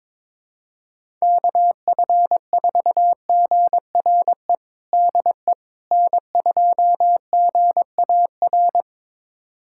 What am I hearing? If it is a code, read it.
Morse code, heard as a keyed tone, about 22 wpm, 700 Hz. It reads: KF4GRE DE N2GAR
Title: USA Callsign DE Callsign 22wpm